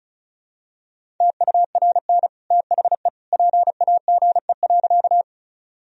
Morse code, heard as a keyed tone, about 36 wpm, 700 Hz. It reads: TURN THE PAGE.
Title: Sentences from Top 1000 Words - Review Part 1 - 35wpm